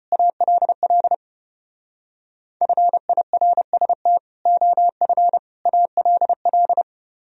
Morse code, heard as a keyed tone, about 32 wpm, 700 Hz. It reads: ALL FIRST OF ALL